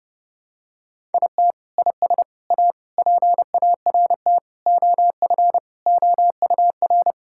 Morse code, heard as a keyed tone, about 32 wpm, 700 Hz. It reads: IT IS A PART OF OUR
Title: Sentences from Top 1000 Words - Review Part 5 - 30wpm